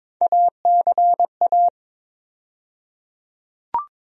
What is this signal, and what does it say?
Morse code, heard as a keyed tone, about 22 wpm, 700 Hz. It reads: ACA
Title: USA Call Sign Postfixes - 3 Letters 22wpm